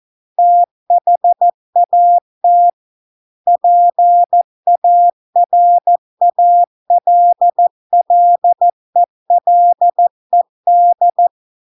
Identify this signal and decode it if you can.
Morse code, heard as a keyed tone, about 14 wpm, 700 Hz. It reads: THAT PARALLELED